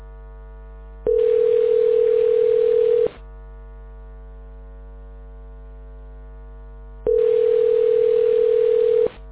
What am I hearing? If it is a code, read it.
Ringback tone (North America), 440 + 480 Hz.